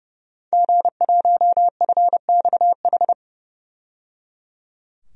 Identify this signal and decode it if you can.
Morse code, heard as a keyed tone, about 30 wpm, 700 Hz. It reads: G1FXH
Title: DXCC-100 Call Signs - 5 Characters - 30wpm